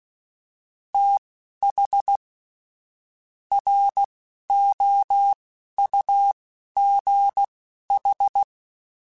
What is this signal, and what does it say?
Morse code, heard as a keyed tone, about 16 wpm, spaced as for 11 wpm, 800 Hz. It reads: TH ROUGH